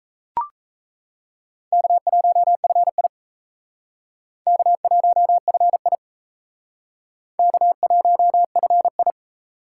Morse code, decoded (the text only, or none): K1FI K1FI K1FI